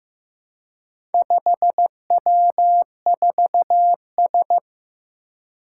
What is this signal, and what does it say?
Morse code, heard as a keyed tone, about 15 wpm, 700 Hz. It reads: 5W4S
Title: Four Challenging Letter-Number Combinations 15wpm